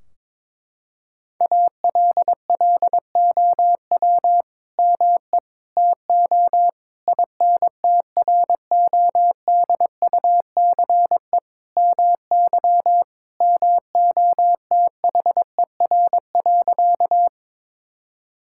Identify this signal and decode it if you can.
Morse code, heard as a keyed tone, about 22 wpm, 700 Hz. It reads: ALLOW ME TO INTRODUCE MY MOTHER.
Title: Sentences from Top 1600 Words 22wpm